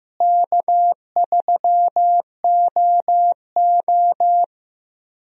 Morse code, heard as a keyed tone, about 15 wpm, 700 Hz. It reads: K3OO
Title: USA Call Signs - 4 Characters - Repeated 3x - 15wpm